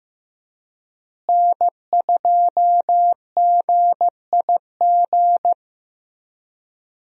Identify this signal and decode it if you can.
Morse code, heard as a keyed tone, about 15 wpm, 700 Hz. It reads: N2GIG
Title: USA Call Signs - 5 Characters - Repeated 3x - 15wpm